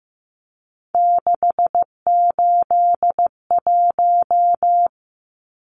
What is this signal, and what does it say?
Morse code, heard as a keyed tone, about 15 wpm, 700 Hz. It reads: 681
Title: Numbers - 3-Digit 15wpm